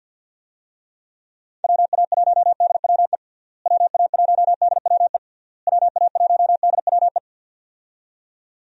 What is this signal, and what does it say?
Morse code, heard as a keyed tone, about 46 wpm, 700 Hz. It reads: WA1DWE WA1DWE WA1DWE
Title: USA Call Signs - 6 Characters - Repeated 3x - 50wpm